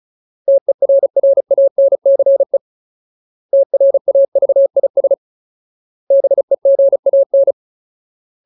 Morse code, heard as a keyed tone, about 34 wpm, 550 Hz. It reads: TERRANCE TRAVIS BEGAN
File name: Ultimate Word List / 35WPM Ultimate